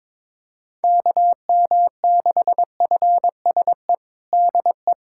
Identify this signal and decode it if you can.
Morse code, heard as a keyed tone, about 21 wpm, 700 Hz. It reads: KM6FSE DE
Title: USA Callsign DE Callsign - 6 Characters - 22wpm